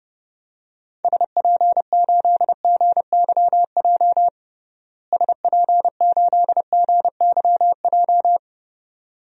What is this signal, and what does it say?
Morse code, heard as a keyed tone, about 32 wpm, 700 Hz. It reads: SP8GYJ SP8GYJ
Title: DXCC-100 Call Signs - 6 Characters - Repeated 2x - 30wpm